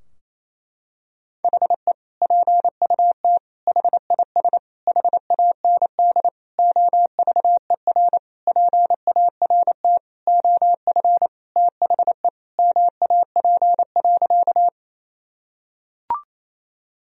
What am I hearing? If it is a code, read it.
Morse code, heard as a keyed tone, about 28 wpm, 700 Hz. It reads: HE PUT HIS HAND OVER PART OF THE MAP.
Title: Sentences from Top 1000 Words - Review Part 2 - 28wpm